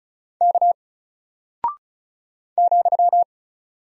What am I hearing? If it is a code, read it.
Morse code, heard as a keyed tone, about 36 wpm, 700 Hz. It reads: K ,
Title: Single Character (Letter:Number:Punctuation:Prosign) Alt 35wpm